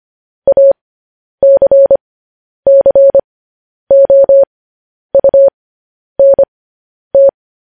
Morse code, heard as a keyed tone, about 25 wpm, spaced as for 10 wpm, 550 Hz. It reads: ACCOUNT
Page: news headlines in morse code at 10 wpm